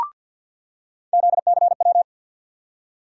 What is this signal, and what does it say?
Morse code, heard as a keyed tone, about 47 wpm, 700 Hz. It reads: ZCW
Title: USA Call Sign Postfixes - 3 Letters 50wpm